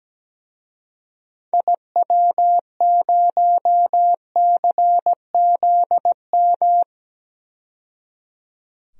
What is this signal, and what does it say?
Morse code, heard as a keyed tone, about 17 wpm, 700 Hz. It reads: IW0CZM
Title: DXCC-100 Call Signs - 6 Characters - 17wpm